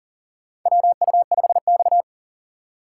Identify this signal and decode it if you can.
Morse code, heard as a keyed tone, about 40 wpm, 700 Hz. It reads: WU5X
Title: USA Call Signs - 4 Characters - 4x Character Spacing - 40wpm